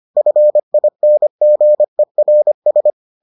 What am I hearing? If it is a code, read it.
Morse code, heard as a keyed tone, about 25 wpm, 600 Hz. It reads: FINGERS